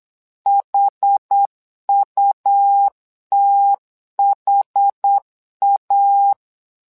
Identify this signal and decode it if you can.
Morse code, heard as a keyed tone, about 8 wpm, 800 Hz. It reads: HUTHA
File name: LEZIONI / 2 UHDN / 2 UHD 8wpm 2258